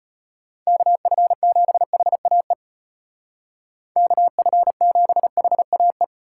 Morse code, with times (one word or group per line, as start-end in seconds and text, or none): 0.7-2.5 KF7HAE
4.0-6.0 KF7HAE